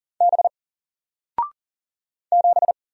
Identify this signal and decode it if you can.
Morse code, heard as a keyed tone, about 40 wpm, 700 Hz. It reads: B 7
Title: Single Letter-Number 40wpm